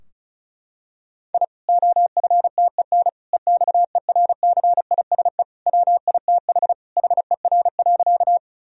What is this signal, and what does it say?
Morse code, heard as a keyed tone, about 36 wpm, 700 Hz. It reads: I OFTEN EXERCISE WITH HER.
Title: Sentences from Top 1000 Words - Review Part 4 - 35wpm